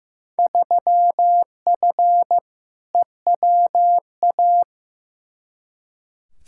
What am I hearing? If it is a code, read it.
Morse code, heard as a keyed tone, about 15 wpm, 700 Hz. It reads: 3F EWA
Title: Field Day Exchanges Encore - 15wpm